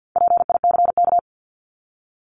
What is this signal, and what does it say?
Morse code, heard as a keyed tone, about 50 wpm, 700 Hz. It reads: PICK
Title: Top 751-1000 Words 50wpm